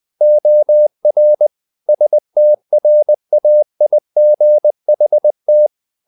Morse code, heard as a keyed tone, about 20 wpm, 600 Hz. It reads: OR STRAIGHT
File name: the old man and the sea part 2 20wpm